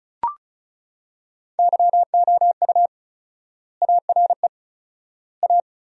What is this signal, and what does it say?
Morse code, heard as a keyed tone, about 34 wpm, 700 Hz. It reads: YOU ARE A
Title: 4x Word-Spacing - Sentences from Top 200 Words - 35wpm